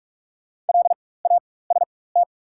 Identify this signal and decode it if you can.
Morse code, heard as a keyed tone, about 48 wpm, spaced as for 21 wpm, 700 Hz. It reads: LAST
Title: 4x Character Spacing - Top 300 Words - 45wpm